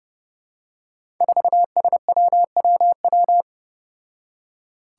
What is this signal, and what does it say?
Morse code, heard as a keyed tone, about 30 wpm, 700 Hz. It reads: 4SWWW
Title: Five Challenging Letter-Number Combinations 30wpm